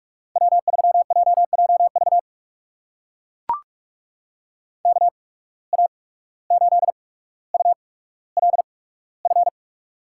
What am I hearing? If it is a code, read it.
Morse code, heard as a keyed tone, about 44 wpm, spaced as for 18 wpm, 690 Hz. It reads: W3JJV KA8ULF
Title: USA Call Signs - 8x Character Spacing - 45wpm